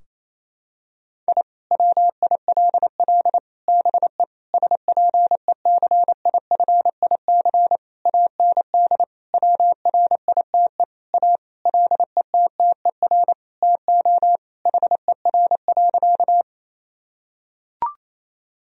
Morse code, heard as a keyed tone, about 28 wpm, 700 Hz. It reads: I WILL BE SPECIFIC AND WRITE A LETTER TO HER.
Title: Sentences from Top 1000 Words - Review Part 3 - 28wpm